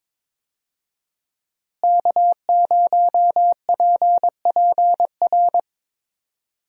Morse code, heard as a keyed tone, about 22 wpm, 700 Hz. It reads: K0PPR
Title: USA Call Signs - 5 Characters - Repeated 3x - 22wpm